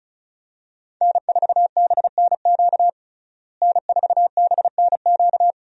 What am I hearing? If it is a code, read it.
Morse code, heard as a keyed tone, about 35 wpm, 700 Hz. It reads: N4BNQ N4BNQ
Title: USA Call Signs - 5 Characters - Repeated 2x - 35wpm